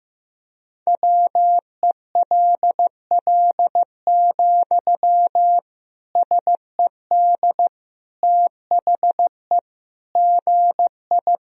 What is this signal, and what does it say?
Morse code, heard as a keyed tone, about 15 wpm, 700 Hz. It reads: WELL, SED THE GI
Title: Sentences from Top 400 Words and Abbreviations 15wpm